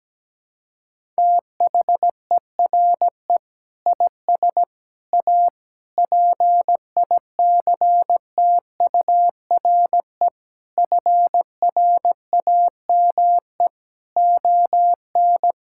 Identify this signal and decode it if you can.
Morse code, heard as a keyed tone, about 17 wpm, 700 Hz. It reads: THERE IS A PICTURE FRAME ON
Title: Sentences from Top 1000 Words - Review Part 6 - 17wpm